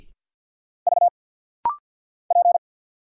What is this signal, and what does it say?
Morse code, heard as a keyed tone, about 50 wpm, 700 Hz. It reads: V P
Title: Single Letter-Number Alt - Rapid-Fire 50wpm